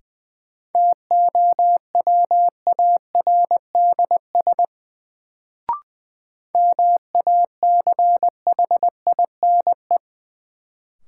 Morse code, heard as a keyed tone, about 20 wpm, 700 Hz. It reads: TOWARDS MACHINE
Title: Sentences from Top 1000 Words - Review Part 2 - New Word Drill - 20wpm